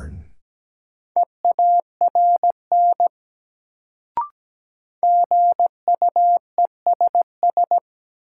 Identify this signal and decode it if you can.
Morse code, heard as a keyed tone, about 17 wpm, 700 Hz. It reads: EARN GUESS
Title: Sentences from Top 1500 Words - New Word Drill - 17wpm